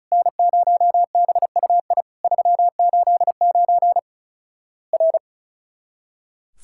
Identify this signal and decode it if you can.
Morse code, heard as a keyed tone, about 36 wpm, 700 Hz. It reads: N0BUI 389 R
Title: Worked All Europe DX Contest - QTC Exchange - 35wpm